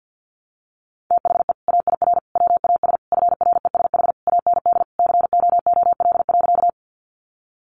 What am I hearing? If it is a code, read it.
Morse code, heard as a keyed tone, about 48 wpm, 700 Hz. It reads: THE AIR WAS FRESH AND COOL.
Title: Sentences from Top 1200 Words 50wpm